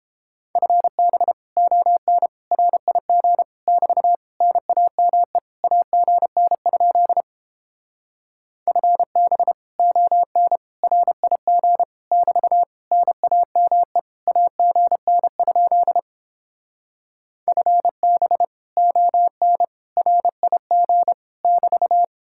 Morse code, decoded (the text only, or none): FB ON RIG <BT> NAME AGN? FB ON RIG <BT> NAME AGN? FB ON RIG <BT>